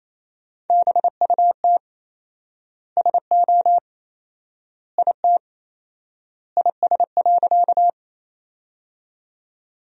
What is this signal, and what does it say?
Morse code, heard as a keyed tone, about 28 wpm, 700 Hz. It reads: BUT SO IT IS.
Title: 4x Word-Spacing - Sentences from Top 100 Words - 28wpm